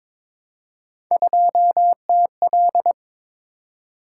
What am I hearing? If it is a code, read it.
Morse code, heard as a keyed tone, about 22 wpm, 700 Hz. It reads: 2TL
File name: practice / 22wpm / Three Letter-Number Combinations - 22wpm